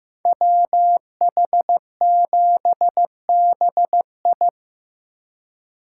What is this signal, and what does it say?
Morse code, heard as a keyed tone, about 18 wpm, 700 Hz. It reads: WH7BI
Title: Speed Racing - USA 2x2 Call Signs 15wpm